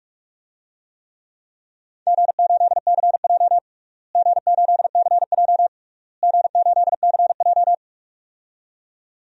Morse code, heard as a keyed tone, about 43 wpm, 700 Hz. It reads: G8CJ G8CJ G8CJ
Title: DXCC-100 Call Signs - 4 Characters - Repeated 3x - 45wpm